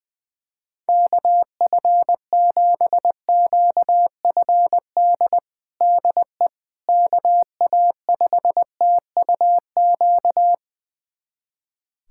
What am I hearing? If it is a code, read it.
Morse code, heard as a keyed tone, about 20 wpm, 700 Hz. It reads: KF7QFD DE KA5TUQ
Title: USA Callsign DE Callsign - 6 Characters - 20wpm